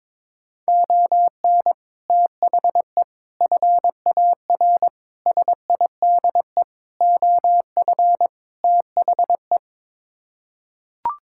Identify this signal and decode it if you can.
Morse code, heard as a keyed tone, about 22 wpm, 700 Hz. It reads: ON THE FAR SIDE OF THE
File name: practice / 22wpm / Sets of 6 Words - 22wpm